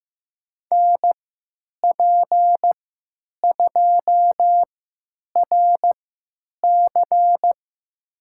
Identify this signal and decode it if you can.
Morse code, heard as a keyed tone, about 15 wpm, spaced as for 9 wpm, 700 Hz. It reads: NP2RC
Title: USA Call Signs - 5 Characters - 3x Character Spacing - 15wpm